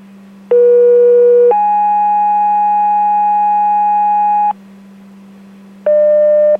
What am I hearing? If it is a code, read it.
Two-tone page, heard as a one-second tone, then a three-second tone, about 480 then 830 Hz. It is heard from 0.5 to 4.5 s.